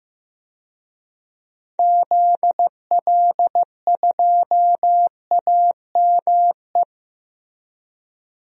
Morse code, heard as a keyed tone, about 15 wpm, 700 Hz. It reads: ZL2AME
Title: DXCC-100 Call Signs - 6 Characters - Repeated 2x - 15wpm